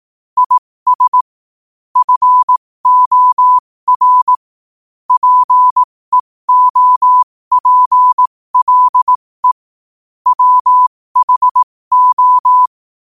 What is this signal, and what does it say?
Morse code, heard as a keyed tone, about 18 wpm, spaced as for 15 wpm, 1000 Hz. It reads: IS FOR PEOPLE WHO
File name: qotdfiles / 15wpm